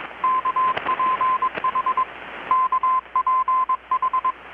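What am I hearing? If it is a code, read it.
Morse code, heard as a keyed tone, about 21 wpm, 1030 Hz, with radio noise. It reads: KPH KPH
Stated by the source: Morse from KPH's Night of Nights XX maritime radiotelegraphy event, received in Chiapas, Mexico about 3,800 km away on a Kenwood R-600 with a grounded 10 m long-wire antenna.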